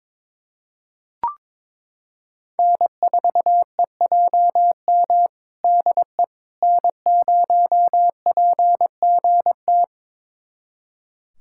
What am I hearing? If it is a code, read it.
Morse code, heard as a keyed tone, about 22 wpm, 700 Hz. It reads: N4EJM DE N0PGT
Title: USA Callsign DE Callsign - 5 Characters - 22wpm